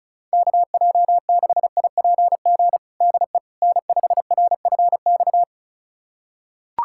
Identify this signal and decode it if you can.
Morse code, heard as a keyed tone, about 36 wpm, 700 Hz. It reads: KJ6IPG DE N5RFX
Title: USA Callsign DE Callsign 35wpm